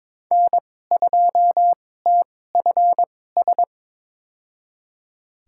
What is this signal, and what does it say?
Morse code, heard as a keyed tone, about 22 wpm, spaced as for 16 wpm, 700 Hz. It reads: N2TFS
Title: USA Call Signs - 5 Characters - 2x Character Spacing - 22wpm